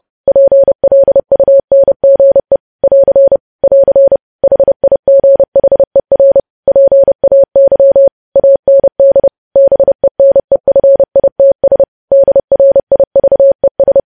Morse code, heard as a keyed tone, about 30 wpm, 550 Hz. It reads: PLUNGE <AR> <AR> HIGHER PAY AND BENEFITS DRIVES
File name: podcast / News headlines-1x-30x30